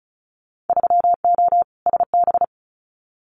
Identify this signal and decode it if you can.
Morse code, heard as a keyed tone, about 36 wpm, 700 Hz. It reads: 3O SB
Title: Winter Field Day Exchanges Encore 35wpm